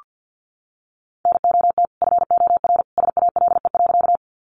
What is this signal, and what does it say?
Morse code, heard as a keyed tone, about 48 wpm, 700 Hz. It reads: NOT FOR SALE.
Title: Sentences from Top 1400 Words 50wpm